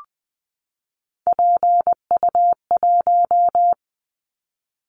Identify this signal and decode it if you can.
Morse code, heard as a keyed tone, about 19 wpm, 700 Hz. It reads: PU1
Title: Callsign Prefixes for Top 100 DXCC - 20wpm